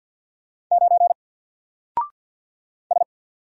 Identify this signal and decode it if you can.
Morse code, heard as a keyed tone, about 50 wpm, 700 Hz. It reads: 9 S
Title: Single Letter-Number 50wpm